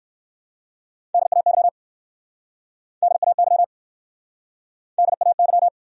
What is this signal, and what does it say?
Morse code, heard as a keyed tone, about 59 wpm, 700 Hz. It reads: DAY DAY DAY